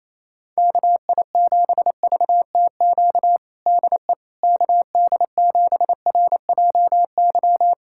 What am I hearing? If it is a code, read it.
Morse code, heard as a keyed tone, about 27 wpm, 700 Hz. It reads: KI7VTQ DE KD7RJY
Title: USA Callsign DE Callsign - 6 Characters - 28wpm